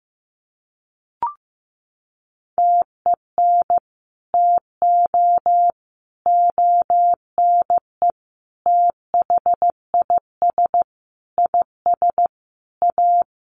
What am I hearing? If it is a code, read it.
Morse code, heard as a keyed tone, about 15 wpm, 700 Hz. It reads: TEN TO ONE THIS IS A
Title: Sentences from Top 1500 Words 15wpm